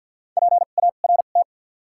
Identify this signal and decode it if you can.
Morse code, heard as a keyed tone, about 52 wpm, 700 Hz. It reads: P A R T
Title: 2x Character Spacing - Top 200 Words - 50wpm